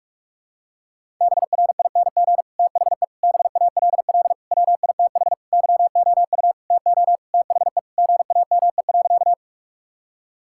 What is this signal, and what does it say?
Morse code, heard as a keyed tone, about 47 wpm, 700 Hz. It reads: BRING THE BALL WITH YOU TO THE GAME.
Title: Sentences from Top 1000 Words - Review Part 6 - 45wpm